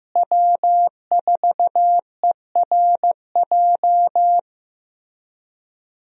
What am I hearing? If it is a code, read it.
Morse code, heard as a keyed tone, about 15 wpm, 700 Hz. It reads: W4ERJ
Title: USA Call Signs - 5 Characters - Repeated 3x - 15wpm